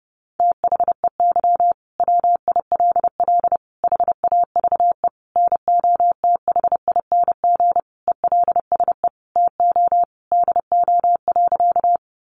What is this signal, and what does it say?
Morse code, heard as a keyed tone, about 30 wpm, 700 Hz. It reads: THEY WILL HAVE NOTHING ELSE TO DO.
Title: Sentences from Top 800 Words 30wpm